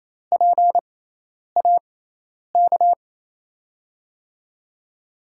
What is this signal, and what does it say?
Morse code, heard as a keyed tone, about 28 wpm, spaced as for 10 wpm, 700 Hz. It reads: PAK
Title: USA Call Signs - 6 Characters - 6x Character Spacing - 28wpm